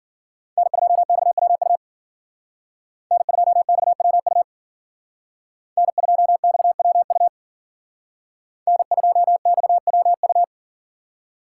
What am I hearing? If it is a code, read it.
Morse code, heard as a keyed tone, about 52 wpm, 700 Hz. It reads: N2XWU N2XWU N2XWU N2XWU